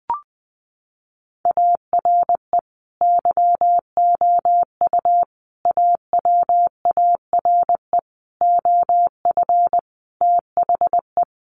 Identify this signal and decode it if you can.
Morse code, heard as a keyed tone, about 20 wpm, 700 Hz. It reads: ARE YOU AWARE OF THE
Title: Sentences from Top 1500 Words 20wpm